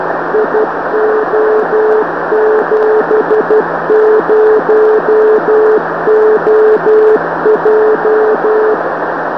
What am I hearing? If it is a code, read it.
Morse code, heard as a keyed tone, about 12 wpm, 430 Hz, with noise. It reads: IO70OJ
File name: gb3mcb-4m